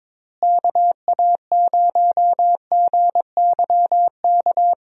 Morse code, heard as a keyed tone, about 22 wpm, 700 Hz. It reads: KA0GYK
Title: USA Call Signs - 6 Characters - Repeated 2x - 22wpm